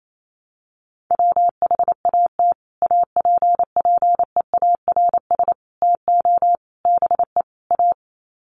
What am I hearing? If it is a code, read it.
Morse code, heard as a keyed tone, about 28 wpm, 700 Hz. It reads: WHAT APPEARS TO BE A
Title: Sets of 5 Words Encore2 28wpm